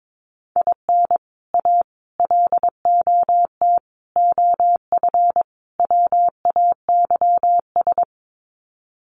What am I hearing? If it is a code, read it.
Morse code, heard as a keyed tone, about 22 wpm, 700 Hz. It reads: IN A LOT OF WAYS
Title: Sets of 5 Words Encore 22wpm